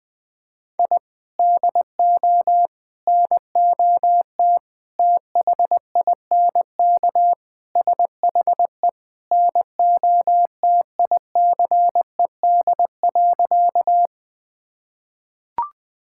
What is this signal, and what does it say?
Morse code, heard as a keyed tone, about 21 wpm, 700 Hz. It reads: I DO NOT THINK SHE NOTICED.
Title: Sentences with 5-Word Phrases - Spoken Beginning - 20wpm